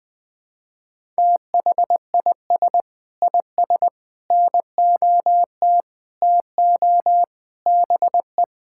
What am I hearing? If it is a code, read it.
Morse code, heard as a keyed tone, about 20 wpm, 700 Hz. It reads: THIS IS NOT TO BE
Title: Sentences with 7-Word Phrases - 20wpm